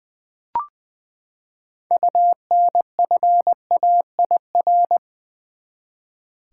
Morse code, heard as a keyed tone, about 20 wpm, 700 Hz. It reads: UNFAIR
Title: Six Letter Words - 20wpm